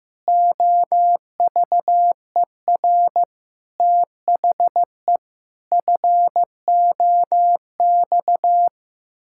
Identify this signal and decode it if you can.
Morse code, heard as a keyed tone, about 15 wpm, 700 Hz. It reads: OVER THE FOX
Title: Pangram Challenge v2 - 15wpm